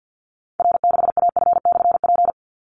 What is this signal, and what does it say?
Morse code, heard as a keyed tone, about 50 wpm, 700 Hz. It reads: R6AF/P
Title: Call Signs Repeated 2x - 50wpm